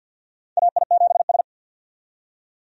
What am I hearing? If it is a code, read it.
Morse code, heard as a keyed tone, about 50 wpm, 700 Hz. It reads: AI7S
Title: USA Call Signs - 4 Characters - 50wpm